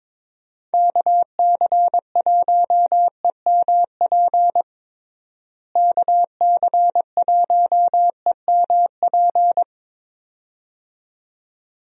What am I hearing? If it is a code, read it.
Morse code, heard as a keyed tone, about 22 wpm, 700 Hz. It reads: KC1EMP KC1EMP
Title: USA Call Signs - 6 Characters - Repeated 2x - 22wpm